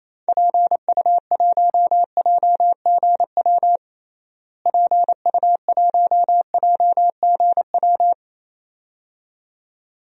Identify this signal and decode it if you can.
Morse code, heard as a keyed tone, about 28 wpm, 700 Hz. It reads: PU1JGW PU1JGW
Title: DXCC-100 Call Signs - 6 Characters - Repeated 2x - 28wpm